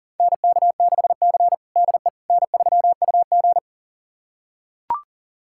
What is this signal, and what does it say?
Morse code, heard as a keyed tone, about 40 wpm, 700 Hz. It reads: NK6C DE N3UG
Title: USA Callsign DE Callsign - 4 Characters - 40wpm